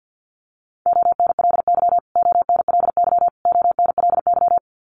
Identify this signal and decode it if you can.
Morse code, heard as a keyed tone, about 49 wpm, 700 Hz. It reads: ONLY ONLY ONLY
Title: Top 200 Words Repeated 50wpm